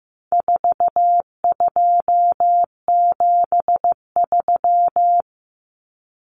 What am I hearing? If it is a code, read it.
Morse code, heard as a keyed tone, about 15 wpm, 700 Hz. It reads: 4273
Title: Numbers - 4-Digit 15wpm